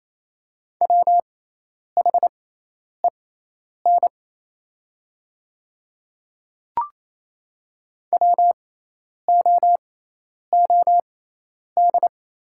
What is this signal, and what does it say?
Morse code, heard as a keyed tone, about 28 wpm, spaced as for 10 wpm, 700 Hz. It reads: WHEN WOOD
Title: 6x Character Spacing - Top 500 Words - 28wpm